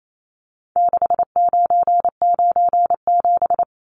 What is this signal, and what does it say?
Morse code, heard as a keyed tone, about 28 wpm, 700 Hz. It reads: 6997
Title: Numbers - 4-Digit 28wpm